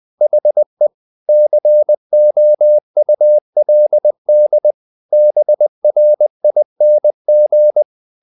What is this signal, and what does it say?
Morse code, heard as a keyed tone, about 20 wpm, 600 Hz. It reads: HE COULD BRING
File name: the old man and the sea part 2 20wpm